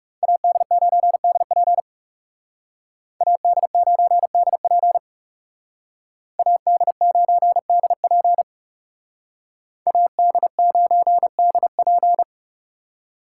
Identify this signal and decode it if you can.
Morse code, heard as a keyed tone, about 39 wpm, 700 Hz. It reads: AD9DP AD9DP AD9DP AD9DP